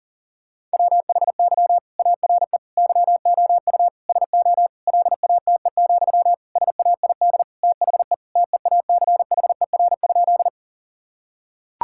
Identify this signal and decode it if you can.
Morse code, heard as a keyed tone, about 40 wpm, 700 Hz. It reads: WHY ARE YOU SO LATE, SAID THE TEACHER?
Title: Sentences from Top 1000 Words - Review Part 6 - 40wpm